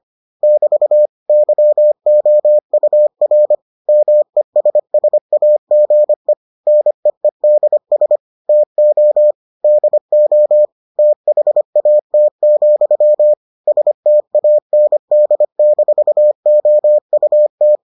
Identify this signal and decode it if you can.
Morse code, heard as a keyed tone, about 25 wpm, 600 Hz. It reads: <BT> YOUR MESSAGE NEEDS TO DO THAT, STAND-OUT